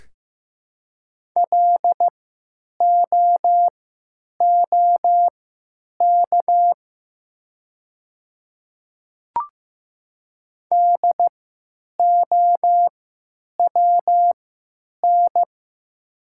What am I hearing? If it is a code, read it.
Morse code, heard as a keyed tone, about 15 wpm, spaced as for 9 wpm, 700 Hz. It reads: LOOK DOWN
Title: 3x Character Spacing - Top 100 Words - 15wpm